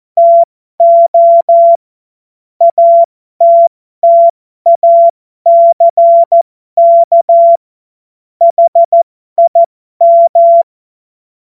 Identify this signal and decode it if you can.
Morse code, heard as a keyed tone, about 14 wpm, spaced as for 12 wpm, 680 Hz. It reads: TO ATTACK HIM
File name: Morsekurs / Witze